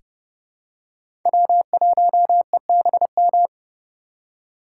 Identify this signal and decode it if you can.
Morse code, heard as a keyed tone, about 30 wpm, 700 Hz. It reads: W1EBM
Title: K1USN SST QSO Practice - 30wpm